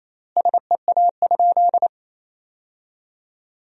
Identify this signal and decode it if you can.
Morse code, heard as a keyed tone, about 28 wpm, 700 Hz. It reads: SEA?
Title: 4x Word-Spacing - Sentences from Top 300 Words - 28wpm